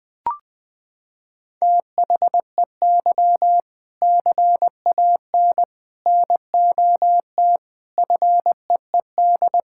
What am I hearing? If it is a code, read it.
Morse code, heard as a keyed tone, about 20 wpm, 700 Hz. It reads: THEY CAN NOT FEED
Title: Sentences from Top 1600 Words 20wpm